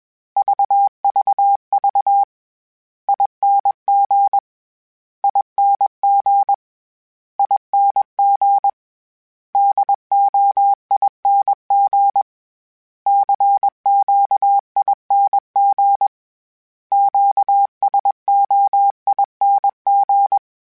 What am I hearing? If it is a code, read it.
Morse code, heard as a keyed tone, about 21 wpm, 800 Hz. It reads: VVV ING ING ING DOING CQING QSOING
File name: Special INGs / Special INGs-mp3s / Sngl Ing @20